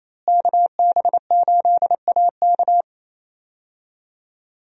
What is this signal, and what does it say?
Morse code, heard as a keyed tone, about 28 wpm, 700 Hz. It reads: KB8AK
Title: USA Call Signs - 5 Characters - Repeated 3x - 28wpm